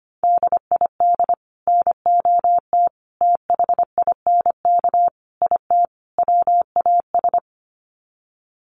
Morse code, heard as a keyed tone, about 25 wpm, 700 Hz. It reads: DID NOT THINK IT WAS
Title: Sets of 5 Words Encore 25wpm